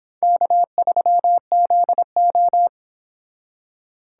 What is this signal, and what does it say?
Morse code, heard as a keyed tone, about 30 wpm, 700 Hz. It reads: K3ZO